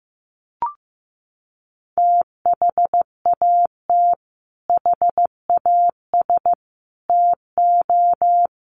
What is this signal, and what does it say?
Morse code, heard as a keyed tone, about 15 wpm, 700 Hz. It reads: THAT HAS TO
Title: Sets of 5 Words Encore2 15wpm